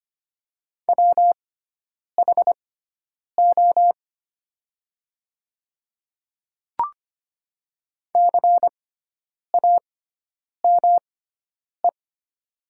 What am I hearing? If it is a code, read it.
Morse code, heard as a keyed tone, about 25 wpm, spaced as for 9 wpm, 700 Hz. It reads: WHO CAME
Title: 6x Character Spacing - Top 300 Words - 25wpm